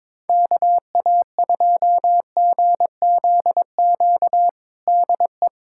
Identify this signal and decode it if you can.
Morse code, heard as a keyed tone, about 21 wpm, 700 Hz. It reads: KA2GZQ DE
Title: USA Callsign DE Callsign 22wpm